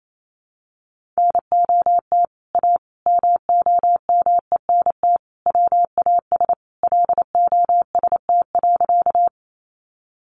Morse code, heard as a keyed tone, about 28 wpm, 700 Hz. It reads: NOT A MOMENT WAS LOST.
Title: Sentences from Top 1400 Words 28wpm